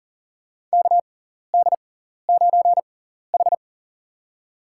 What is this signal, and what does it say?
Morse code, heard as a keyed tone, about 40 wpm, spaced as for 14 wpm, 700 Hz. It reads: KD9H
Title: USA Call Signs - 4 Characters - 6x Character Spacing - 40wpm